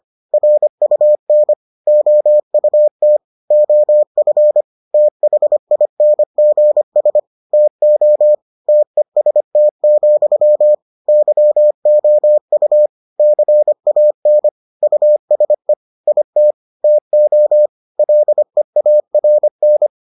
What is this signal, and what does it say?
Morse code, heard as a keyed tone, about 25 wpm, 600 Hz. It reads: RUN OUT OF THINGS TO TEST, YOU CAN USE IT TO LEARN